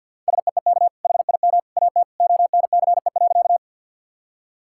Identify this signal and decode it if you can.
Morse code, heard as a keyed tone, about 50 wpm, 700 Hz. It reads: SEEK HIM AT ONCE.
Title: Sentences from Top 1600 Words 50wpm